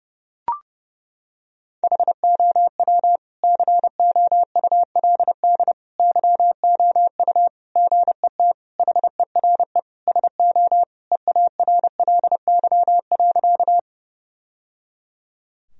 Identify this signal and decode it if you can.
Morse code, heard as a keyed tone, about 30 wpm, 700 Hz. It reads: HOW COULD YOU GET HERE SO EARLY.
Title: Sentences from Top 1000 Words - Review Part 2 - 30wpm